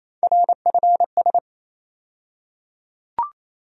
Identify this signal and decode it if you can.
Morse code, heard as a keyed tone, about 28 wpm, 700 Hz. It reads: RFS
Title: USA Call Sign Postfixes - 3 Letters 28wpm